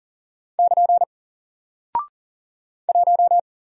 Morse code, heard as a keyed tone, about 39 wpm, 700 Hz. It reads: <KN> 1
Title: Single Character (Letter:Number:Punctuation:Prosign) Alt 40wpm